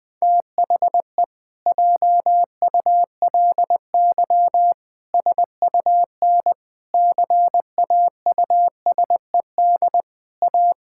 Morse code, heard as a keyed tone, about 20 wpm, 700 Hz. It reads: THE JULY SUN CAUSED A
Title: Pangram Challenge v2 - 20wpm